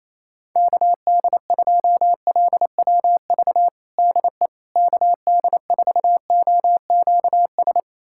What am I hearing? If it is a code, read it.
Morse code, heard as a keyed tone, about 28 wpm, 700 Hz. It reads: KD2LWV DE KD4OQS
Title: USA Callsign DE Callsign - 6 Characters - 28wpm